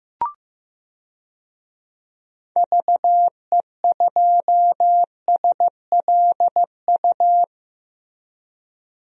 Morse code, heard as a keyed tone, about 15 wpm, 700 Hz. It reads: VE2SLU
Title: DXCC-100 Call Signs - 6 Characters - 15wpm